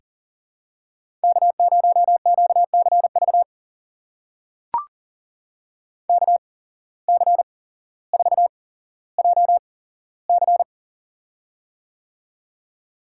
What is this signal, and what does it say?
Morse code, heard as a keyed tone, about 40 wpm, spaced as for 17 wpm, 700 Hz. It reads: K0QCV KC4JC
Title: USA Call Signs - 5 Characters - 8x Character Spacing - 40wpm